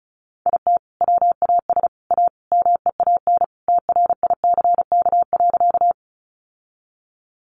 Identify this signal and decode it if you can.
Morse code, heard as a keyed tone, about 34 wpm, 700 Hz. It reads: IT WAS A MEAN TRICK.
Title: Sentences from Top 1500 Words 35wpm